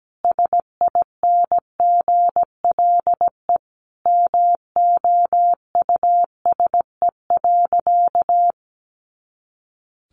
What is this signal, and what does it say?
Morse code, heard as a keyed tone, about 17 wpm, 700 Hz. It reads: SINGLE MOUSE.
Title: A Sound Of Thunder 17wpm